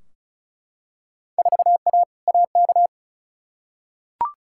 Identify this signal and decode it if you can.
Morse code, heard as a keyed tone, about 34 wpm, 700 Hz. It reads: 4A AK
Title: Field Day Exchanges - 35wpm